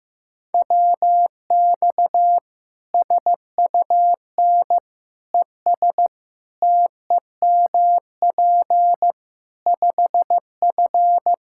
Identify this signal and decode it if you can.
Morse code, heard as a keyed tone, about 15 wpm, 700 Hz. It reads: WX SUN ES TEMP 5F
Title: Practice QSO - Exchange 2 - Info - 15wpm